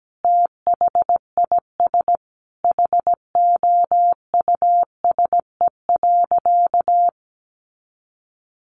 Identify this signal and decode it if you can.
Morse code, heard as a keyed tone, about 17 wpm, 700 Hz. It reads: THIS HOUSE.
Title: Sentences from Top 400 Words 17wpm